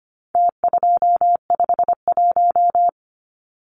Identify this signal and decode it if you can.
Morse code, heard as a keyed tone, about 25 wpm, 700 Hz. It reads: T251
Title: Numbers - Contest Style - 4-Digit 25wpm